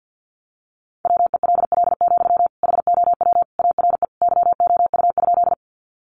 Morse code, heard as a keyed tone, about 50 wpm, 700 Hz. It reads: WELL, HOW ARE YOU?
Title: Sentences from Top 200 Words 50wpm